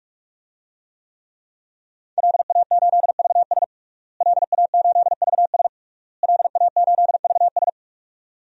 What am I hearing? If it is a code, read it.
Morse code, heard as a keyed tone, about 46 wpm, 700 Hz. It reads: LA8VS LA8VS LA8VS
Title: DXCC-100 Call Signs - 5 Characters - Repeated 3x - 45wpm